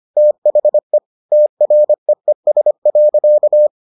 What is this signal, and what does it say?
Morse code, heard as a keyed tone, about 25 wpm, 600 Hz. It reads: THE TREES.